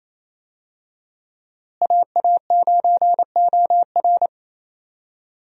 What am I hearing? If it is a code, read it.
Morse code, heard as a keyed tone, about 28 wpm, 700 Hz. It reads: AA9OR
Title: USA Call Signs - 5 Characters - Repeated 3x - 28wpm